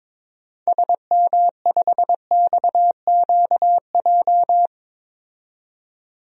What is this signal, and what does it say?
Morse code, heard as a keyed tone, about 22 wpm, 700 Hz. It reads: SM5XQJ
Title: DXCC-100 Call Signs - 6 Characters - 22wpm